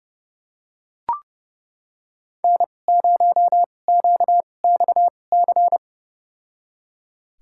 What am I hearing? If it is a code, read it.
Morse code, heard as a keyed tone, about 31 wpm, spaced as for 22 wpm, 700 Hz. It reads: N0QXC
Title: USA Call Signs - 5 Characters - 2x Character Spacing - 30wpm